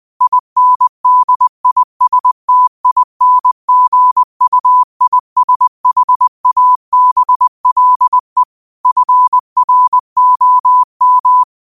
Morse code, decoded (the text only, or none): INDISTINGUISHABLE FROM